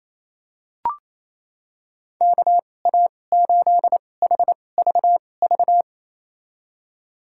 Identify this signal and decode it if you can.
Morse code, heard as a keyed tone, about 28 wpm, spaced as for 20 wpm, 700 Hz. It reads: KA8HVV
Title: USA Call Signs - 2x Character Spacing - 28wpm